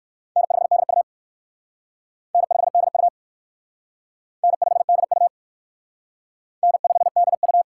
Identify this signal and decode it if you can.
Morse code, heard as a keyed tone, about 60 wpm, 700 Hz. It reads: N5DU N5DU N5DU N5DU